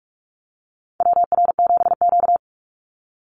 Morse code, heard as a keyed tone, about 44 wpm, 700 Hz. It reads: WR7Q
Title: Call Signs - 45wpm